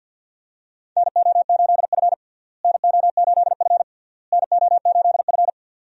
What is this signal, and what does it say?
Morse code, heard as a keyed tone, about 52 wpm, 700 Hz. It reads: NO8F NO8F NO8F
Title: USA Call Signs - 4 Characters - Repeated 3x - 50wpm